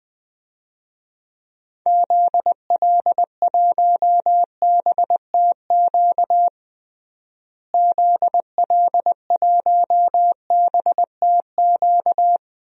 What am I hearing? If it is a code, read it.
Morse code, heard as a keyed tone, about 20 wpm, 700 Hz. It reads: ZL1BTQ ZL1BTQ
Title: DXCC-100 Call Signs - 6 Characters - Repeated 2x - 20wpm